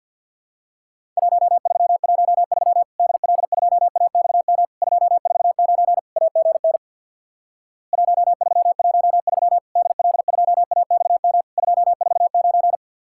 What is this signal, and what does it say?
Morse code, heard as a keyed tone, about 50 wpm, 700 Hz. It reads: 1313 DL2AXM 249 AGN 1313 DL2AXM 249
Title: Worked All Europe DX Contest - QTC Exchange with Repeats - 50wpm